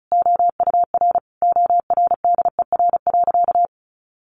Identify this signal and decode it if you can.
Morse code, heard as a keyed tone, about 34 wpm, 700 Hz. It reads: OUR ORDER.
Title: Sentences from Top 1300 Words 35wpm